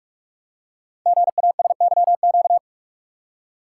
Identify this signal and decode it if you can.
Morse code, heard as a keyed tone, about 44 wpm, 700 Hz. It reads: GASYQ
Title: Code Groups - 5 Letter:Numbers - 45wpm